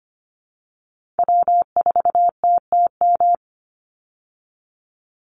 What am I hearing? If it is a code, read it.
Morse code, heard as a keyed tone, about 25 wpm, 700 Hz. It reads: W4TTM
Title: Call Signs Repeated 3x - 25wpm